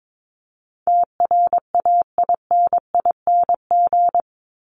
Morse code, heard as a keyed tone, about 22 wpm, 700 Hz. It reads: TRAINING
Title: Top 501-750 Words 22wpm